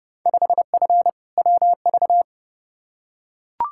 Morse code, heard as a keyed tone, about 30 wpm, 700 Hz. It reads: 5F WV
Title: Field Day Exchanges Encore - 30wpm